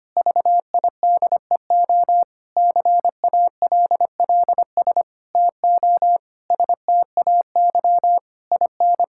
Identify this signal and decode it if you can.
Morse code, heard as a keyed tone, about 25 wpm, 700 Hz. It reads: VIDEO CALLS TO STAY IN
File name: practice / 25wpm / Difficult Sentences with Hints - 25wpm